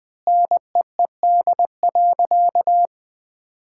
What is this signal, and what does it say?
Morse code, heard as a keyed tone, about 20 wpm, 700 Hz. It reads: NEED.
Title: Sentences from Top 1000 Words - Review Part 5 - 20wpm